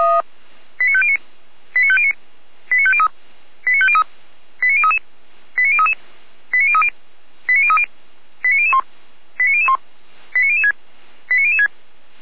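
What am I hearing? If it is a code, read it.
Five-tone selective call (ZVEI), heard as quick runs of five tones, tones of about 70 ms. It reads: 89509 89509 89503 89503 89033 89033 89030 89030 89001 89001 89006 89006
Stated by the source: ZVEI five-tone call 89014 received on SDR.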